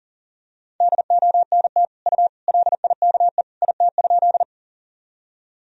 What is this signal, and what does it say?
Morse code, heard as a keyed tone, about 40 wpm, 700 Hz. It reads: DONT U LIKE IT?
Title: Sentences from Top 300 Words and Abbreviations 40wpm